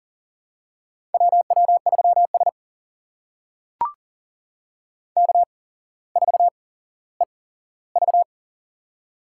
Morse code, heard as a keyed tone, about 42 wpm, spaced as for 17 wpm, 700 Hz. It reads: WW3S K4EV
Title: USA Call Signs - 4 Characters - 8x Character Spacing - 40wpm